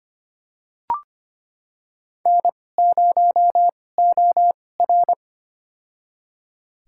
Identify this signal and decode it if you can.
Morse code, heard as a keyed tone, about 25 wpm, spaced as for 18 wpm, 700 Hz. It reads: N0OR
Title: USA Call Signs - 2x Character Spacing - 25wpm